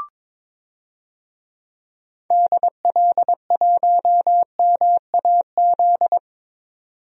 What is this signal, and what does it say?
Morse code, heard as a keyed tone, about 22 wpm, 700 Hz. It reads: DL1MAZ
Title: DXCC-100 Call Signs - 6 Characters - 22wpm